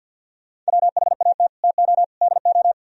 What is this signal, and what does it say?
Morse code, heard as a keyed tone, about 65 wpm, spaced as for 57 wpm, 700 Hz. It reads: WHAT TO DO
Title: Speed-Racing - Sets of 3 Words 50wpm